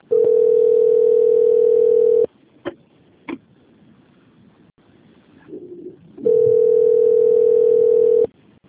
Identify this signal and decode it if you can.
Ringback tone (North America), 440 + 480 Hz.